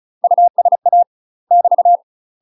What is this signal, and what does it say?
Morse code, heard as a keyed tone, about 36 wpm, 700 Hz. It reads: USA <BT>
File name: UK MASTER QSOs TRAINING / 35WPM / UK MASTER QSOs 0038